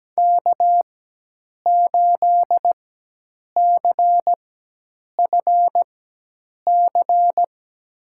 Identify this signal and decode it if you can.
Morse code, heard as a keyed tone, about 17 wpm, spaced as for 8 wpm, 700 Hz. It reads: K8CFC
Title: USA Call Signs - 5 Characters - 4x Character Spacing - 17wpm